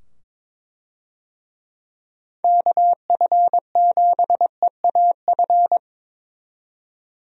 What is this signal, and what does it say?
Morse code, heard as a keyed tone, about 22 wpm, 700 Hz. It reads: KF7EAF
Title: USA Call Signs - 6 Characters - 22wpm